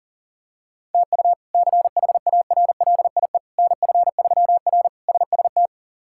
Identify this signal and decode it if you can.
Morse code, heard as a keyed tone, about 37 wpm, 700 Hz. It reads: TU CHARLIE NF3R SST
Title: K1USN SST QSO Practice - 40wpm